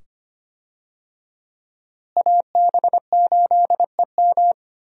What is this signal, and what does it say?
Morse code, heard as a keyed tone, about 25 wpm, 700 Hz. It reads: AB8EM
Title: USA Call Signs - 5 Characters - Repeated 2x - 25wpm